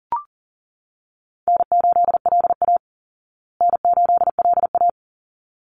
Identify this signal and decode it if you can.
Morse code, heard as a keyed tone, about 40 wpm, 700 Hz. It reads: N8LA N8LA
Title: Easy Call Signs Repeated 2x - 40wpm